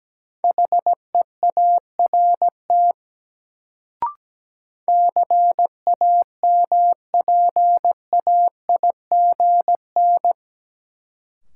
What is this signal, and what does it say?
Morse code, heard as a keyed tone, about 17 wpm, 700 Hz. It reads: HEART CAMPAIGN
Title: Sentences from Top 1200 Words - New Word Drill - 17wpm